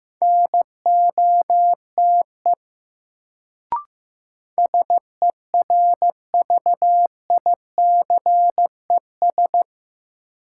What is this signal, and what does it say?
Morse code, heard as a keyed tone, about 16 wpm, 700 Hz. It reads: NOTE SERVICES
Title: Sets of 5 Words - New Word Drill - 15wpm